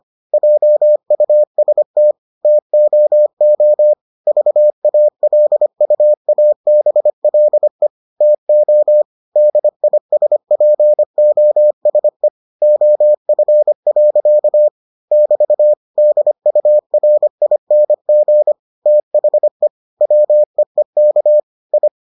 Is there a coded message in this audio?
Yes: Morse code from 0.3 to 21.9 s, heard as a keyed tone, about 25 wpm, 600 Hz. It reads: JUST TOO VALUABLE TO DISPOSE OF. <BT> DURING THE WEEK I